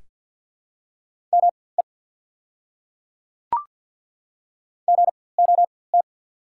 Morse code, heard as a keyed tone, about 51 wpm, spaced as for 23 wpm, 690 Hz. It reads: ME GOT
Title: 4x Character Spacing - Top 300 Words - 50wpm